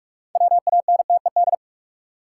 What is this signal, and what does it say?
Morse code, heard as a keyed tone, about 41 wpm, 700 Hz. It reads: WANTED
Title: Sets of 2 Words - New Word Drill - 45wpm